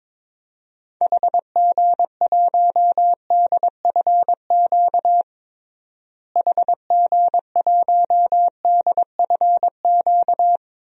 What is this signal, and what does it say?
Morse code, heard as a keyed tone, about 22 wpm, 700 Hz. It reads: HG1DFQ HG1DFQ
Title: DXCC-100 Call Signs - 6 Characters - Repeated 3x - 22wpm